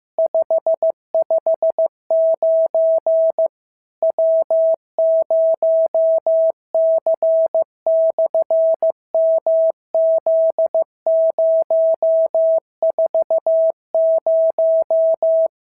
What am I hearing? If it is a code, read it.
Morse code, heard as a keyed tone, about 15 wpm, 650 Hz. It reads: 559 W0C/MZ040
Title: SOTA QSO Practice 15wpm